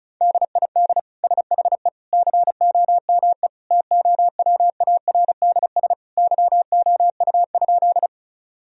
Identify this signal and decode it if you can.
Morse code, heard as a keyed tone, about 36 wpm, 700 Hz. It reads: DID SHE COME TOWARDS YOU?
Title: Sentences from Top 1000 Words - Review Part 2 - 35wpm